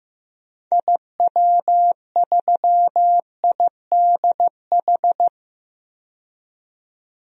Morse code, heard as a keyed tone, about 15 wpm, 700 Hz. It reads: IW3IDH
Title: DXCC-100 Call Signs - 6 Characters - 15wpm